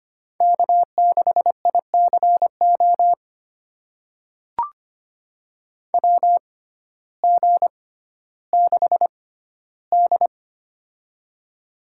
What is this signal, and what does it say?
Morse code, heard as a keyed tone, about 25 wpm, spaced as for 14 wpm, 700 Hz. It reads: K6ICO WG6D
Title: USA Call Signs - 6x Character Spacing - 25wpm